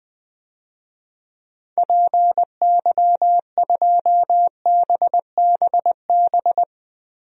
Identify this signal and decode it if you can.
Morse code, heard as a keyed tone, about 20 wpm, 700 Hz. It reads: PY2BBB
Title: DXCC-100 Call Signs - 6 Characters - Repeated 2x - 20wpm